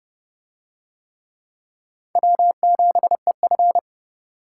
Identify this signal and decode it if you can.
Morse code, heard as a keyed tone, about 30 wpm, 700 Hz. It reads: W7EF